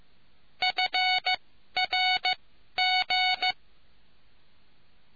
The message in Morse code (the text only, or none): FRG